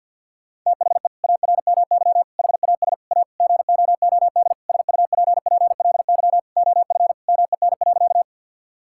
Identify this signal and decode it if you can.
Morse code, heard as a keyed tone, about 50 wpm, 700 Hz. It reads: THE ARMY HAS A GOOD SUPPLY OF MEN.
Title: Sentences from Top 1000 Words - Review Part 5 - 50wpm